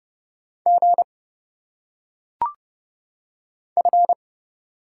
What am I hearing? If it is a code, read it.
Morse code, heard as a keyed tone, about 30 wpm, 700 Hz. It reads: G F
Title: Single Letters - Phonetic Pronunciation 30wpm